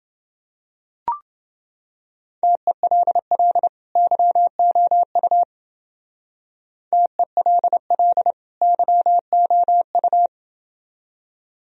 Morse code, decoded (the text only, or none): TELL YOU TELL YOU